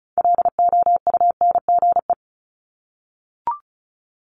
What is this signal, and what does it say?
Morse code, heard as a keyed tone, about 36 wpm, 700 Hz. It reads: LOUNGE
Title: Multi-letter Phonemes Repeated 3x 35wpm